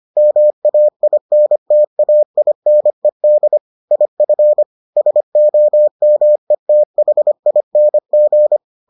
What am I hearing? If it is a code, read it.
Morse code, heard as a keyed tone, about 25 wpm, 600 Hz. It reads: MAINTAINED IF SOMETHING